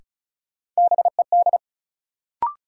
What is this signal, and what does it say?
Morse code, heard as a keyed tone, about 36 wpm, 700 Hz. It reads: BED
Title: Sentences from Top 500 Words - New Word Drill - 35wpm